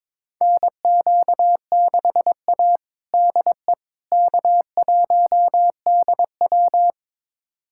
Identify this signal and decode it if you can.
Morse code, heard as a keyed tone, about 22 wpm, 700 Hz. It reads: NQ6A DE K1DW
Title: USA Callsign DE Callsign - 4 Characters - 22wpm